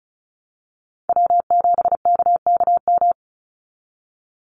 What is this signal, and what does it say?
Morse code, heard as a keyed tone, about 34 wpm, 700 Hz. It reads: W7KKM
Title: NA SOTA Chaser Callsigns 35wpm